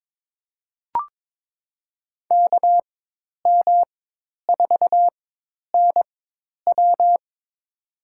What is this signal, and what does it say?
Morse code, heard as a keyed tone, about 22 wpm, spaced as for 10 wpm, 700 Hz. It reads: KM4NW
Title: USA Call Signs - 5 Characters - 4x Character Spacing - 22wpm